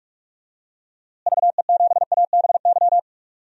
Morse code, heard as a keyed tone, about 45 wpm, 700 Hz. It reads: VE7ABY
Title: DXCC-100 Call Signs - 6 Characters - Repeated 2x - 45wpm